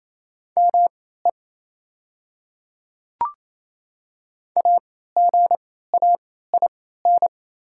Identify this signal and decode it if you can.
Morse code, heard as a keyed tone, about 28 wpm, spaced as for 16 wpm, 690 Hz. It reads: ME AGAIN
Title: 3x Character Spacing - Top 200 Words - 28wpm